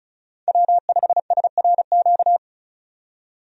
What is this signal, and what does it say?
Morse code, heard as a keyed tone, about 36 wpm, 700 Hz. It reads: W5SRQ
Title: USA Call Signs - 5 Characters - 35wpm